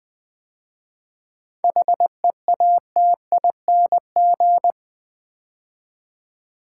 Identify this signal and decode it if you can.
Morse code, heard as a keyed tone, about 20 wpm, 690 Hz. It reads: HEATING
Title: 7 Letter Words - First 3 Letters Given - 20wpm